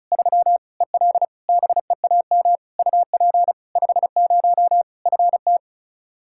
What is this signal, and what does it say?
Morse code, heard as a keyed tone, about 36 wpm, 700 Hz. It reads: 3 EL BEAM UP 50 FT
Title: Practice QSO - Exchange 2 - Info - 35wpm